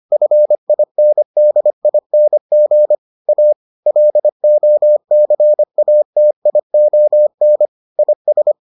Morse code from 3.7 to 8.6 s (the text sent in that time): LOCATION IS